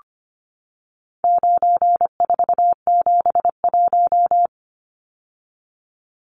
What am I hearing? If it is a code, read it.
Morse code, heard as a keyed tone, about 25 wpm, 700 Hz. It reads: 9471
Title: Numbers - 4-Digit 25wpm